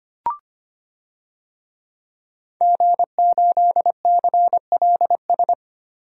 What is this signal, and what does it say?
Morse code, heard as a keyed tone, about 25 wpm, 700 Hz. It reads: G8CLS
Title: DXCC-100 Call Signs - 5 Characters - Repeated 3x - 25wpm